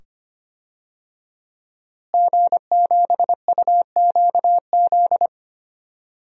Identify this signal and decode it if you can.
Morse code, heard as a keyed tone, about 25 wpm, 700 Hz. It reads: G7UQZ
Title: DXCC-100 Call Signs - 5 Characters - Repeated 2x - 25wpm